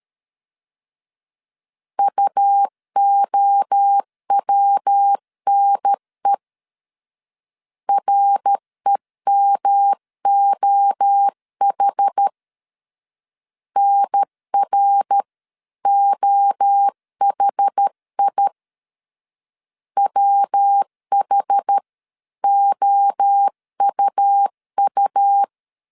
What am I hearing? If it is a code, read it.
Morse code, heard as a keyed tone, about 13 wpm, 800 Hz. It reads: UOWNE REMOH NROHI WHOUU
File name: CW / DET / 1 3 1500C S A 12wpm 2905